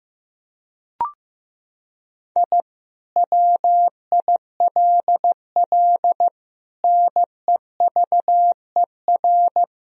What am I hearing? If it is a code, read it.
Morse code, heard as a keyed tone, about 15 wpm, 700 Hz. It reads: I WILL NEVER
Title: Sentences from Top 1000 Words - Review Part 1 - 15wpm